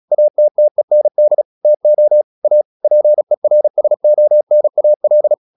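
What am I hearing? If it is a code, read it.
Morse code, heard as a keyed tone, about 36 wpm, 600 Hz. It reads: ATTEND TO A PERSONAL
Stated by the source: Morse practice at 36 wpm, text of Ken Follett's novel Never, from CW med Gnister.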